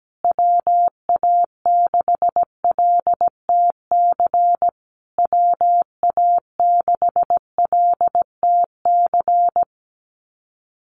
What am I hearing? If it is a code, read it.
Morse code, heard as a keyed tone, about 17 wpm, 700 Hz. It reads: WA6LTC WA6LTC
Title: USA Call Signs Repeated 3x 17wpm